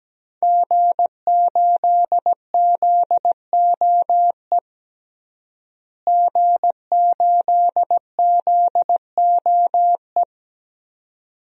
Morse code, heard as a keyed tone, about 17 wpm, 700 Hz. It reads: G8ZOE G8ZOE
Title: DXCC-100 Call Signs - 5 Characters - Repeated 2x - 17wpm